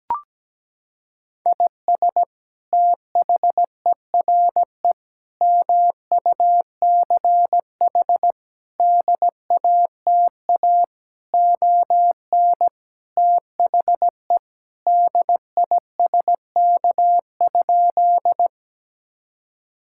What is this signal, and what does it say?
Morse code, heard as a keyed tone, about 17 wpm, 700 Hz. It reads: IS THERE MUCH DATA ON THE DISK?
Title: Sentences from Top 1000 Words - Review Part 6 - 17wpm